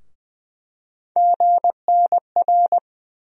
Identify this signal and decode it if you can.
Morse code, heard as a keyed tone, about 20 wpm, 700 Hz. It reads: GNR
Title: USA Call Sign Postfixes - 3 Letters 20wpm